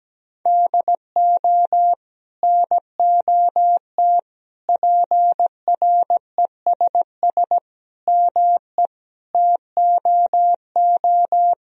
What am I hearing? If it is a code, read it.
Morse code, heard as a keyed tone, about 17 wpm, 700 Hz. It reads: DO NOT PRESS ME TOO
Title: Sentences from Top 1000 Words - Review Part 1 - 17wpm